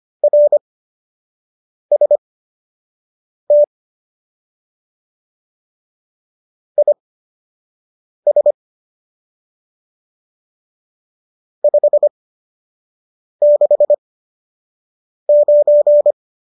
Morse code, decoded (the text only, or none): RST IS 569